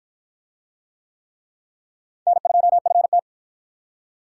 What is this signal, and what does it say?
Morse code, heard as a keyed tone, about 46 wpm, 700 Hz. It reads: N2FT